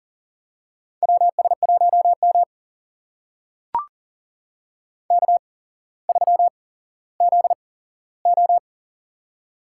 Morse code, heard as a keyed tone, about 40 wpm, spaced as for 17 wpm, 700 Hz. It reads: WS1M K3ZO
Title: USA Call Signs - 4 Characters - 8x Character Spacing - 40wpm